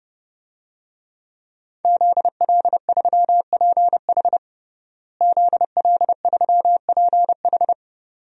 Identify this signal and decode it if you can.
Morse code, heard as a keyed tone, about 30 wpm, 700 Hz. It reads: ZL3PH ZL3PH
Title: DXCC-100 Call Signs - 5 Characters - Repeated 2x - 30wpm